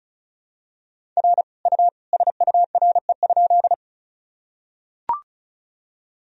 Morse code, heard as a keyed tone, about 36 wpm, 700 Hz. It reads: R U SURE?
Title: Sentences from Top 300 Words and Abbreviations 35wpm